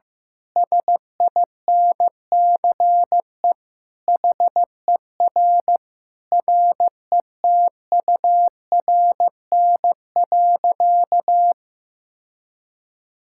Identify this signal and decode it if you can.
Morse code, heard as a keyed tone, about 16 wpm, 700 Hz. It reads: SINCE HER RETURN.
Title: Sentences from Top 1000 Words - Review Part 6 - 15wpm